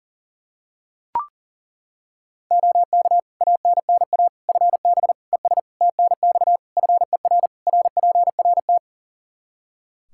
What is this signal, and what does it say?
Morse code, heard as a keyed tone, about 40 wpm, 700 Hz. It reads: OK ANNA FB ES TNX FER RPRT
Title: Practice QSO - Exchange 2 - Info - 40wpm